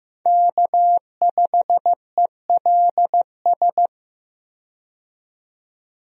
Morse code, heard as a keyed tone, about 15 wpm, 700 Hz. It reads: K5ELS
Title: USA Call Signs - 5 Characters - 15wpm